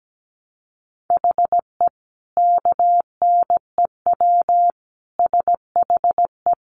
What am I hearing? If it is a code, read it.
Morse code, heard as a keyed tone, about 17 wpm, 700 Hz. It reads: HE KNEW SHE
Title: Sentences from Top 1300 Words 17wpm